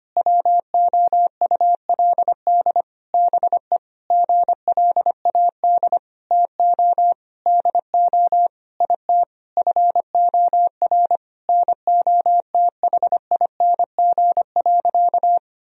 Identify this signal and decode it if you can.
Morse code, heard as a keyed tone, about 25 wpm, 700 Hz. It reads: WOULD BE GLAD TO DO IT FOR NOTHING.
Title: Sentences from Top 1000 Words - Review Part 3 - 25wpm